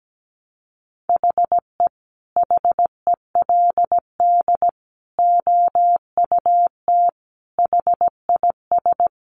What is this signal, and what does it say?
Morse code, heard as a keyed tone, about 17 wpm, 700 Hz. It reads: HE HELD OUT HIS
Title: A Sound Of Thunder 17wpm